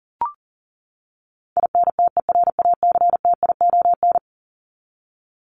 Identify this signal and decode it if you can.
Morse code, heard as a keyed tone, about 40 wpm, 700 Hz. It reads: INTERACTION
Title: Top 1251-1500 Words 40wpm